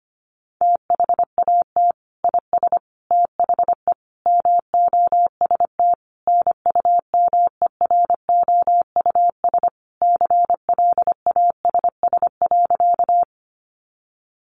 Morse code, heard as a keyed tone, about 25 wpm, 700 Hz. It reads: THAT IS THE MOST NUMEROUS CLASS.
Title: Sentences from Top 1400 Words 25wpm